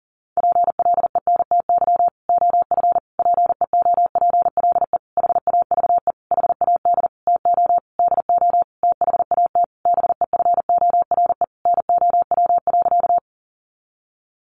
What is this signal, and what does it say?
Morse code, heard as a keyed tone, about 40 wpm, 700 Hz. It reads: PLENTY OF PEOPLE HAVE HAD TO DO THAT BEFORE NOW.
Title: Sentences from Top 1300 Words 40wpm